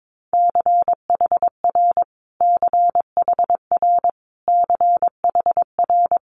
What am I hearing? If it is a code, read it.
Morse code, heard as a keyed tone, about 22 wpm, 700 Hz. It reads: CHR CHR CHR
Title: Multi-letter Phonemes Repeated 3x 22wpm